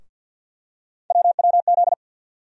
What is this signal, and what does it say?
Morse code, heard as a keyed tone, about 48 wpm, 700 Hz. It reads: WWZ
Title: USA Call Sign Postfixes - 3 Letters 50wpm